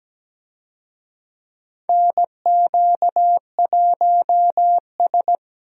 Morse code, heard as a keyed tone, about 17 wpm, 700 Hz. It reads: NQ1S
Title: USA Call Signs - 4 Characters - Repeated 2x - 17wpm